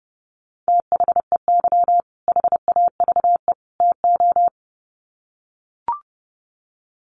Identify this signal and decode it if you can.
Morse code, heard as a keyed tone, about 30 wpm, 700 Hz. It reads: THEY HAVE TO
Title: Sets of 3 Words 30wpm